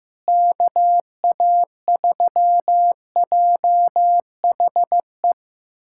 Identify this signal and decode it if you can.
Morse code, heard as a keyed tone, about 15 wpm, 700 Hz. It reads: KA3JHE
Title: USA Callsign DE Callsign - 6 Characters - 15wpm